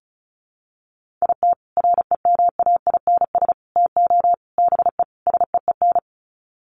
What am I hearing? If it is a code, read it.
Morse code, heard as a keyed tone, about 34 wpm, 700 Hz. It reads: IT REMAINS TO BE SEEN
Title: Sets of 5 Words Encore 35wpm